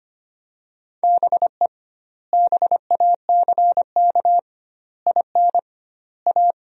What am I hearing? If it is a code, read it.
Morse code, heard as a keyed tone, about 25 wpm, 700 Hz. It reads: BE BACK IN A
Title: 2x Word-Spacing - Sentences from Top 500 Words - 25wpm